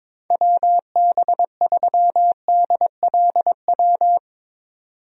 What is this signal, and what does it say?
Morse code, heard as a keyed tone, about 22 wpm, 700 Hz. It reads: WB3DLW
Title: USA Call Signs - 6 Characters - 22wpm